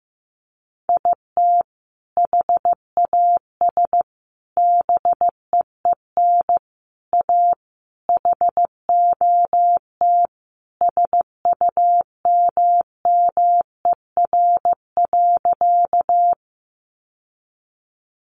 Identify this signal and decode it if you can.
Morse code, heard as a keyed tone, about 15 wpm, 700 Hz. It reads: IT HAS BEEN A HOT SUMMER.
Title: Sentences from Top 1100 Words 15wpm